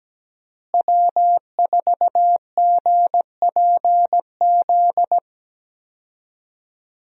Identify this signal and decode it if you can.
Morse code, heard as a keyed tone, about 17 wpm, 700 Hz. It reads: W4GPZ
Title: USA Call Signs - 5 Characters - Repeated 3x - 17wpm